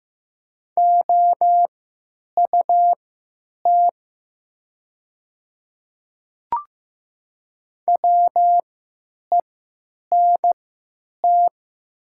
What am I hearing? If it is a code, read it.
Morse code, heard as a keyed tone, about 15 wpm, spaced as for 9 wpm, 700 Hz. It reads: OUT WENT
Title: 3x Character Spacing - Top 200 Words - 15wpm